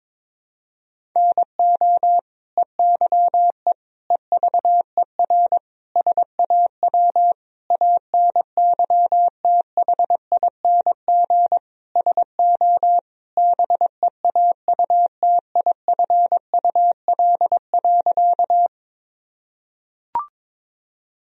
Morse code, heard as a keyed tone, about 21 wpm, 700 Hz. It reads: NO EYE EVER SAW ANYTHING SO BEAUTIFUL.
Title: Sentences from Top 1000 Words - Review Part 1 - 22wpm